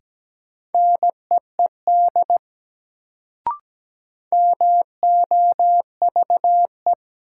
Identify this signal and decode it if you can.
Morse code, heard as a keyed tone, about 17 wpm, 700 Hz. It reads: NEED MOVE
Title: Sentences from Top 200 Words - New Word Drill - 17wpm